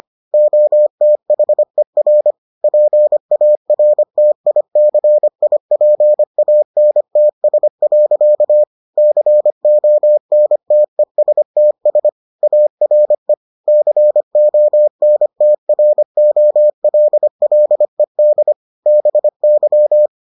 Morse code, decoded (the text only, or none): OTHER PARTICIPANTS. CONTESTS ARE CONTROLLED BY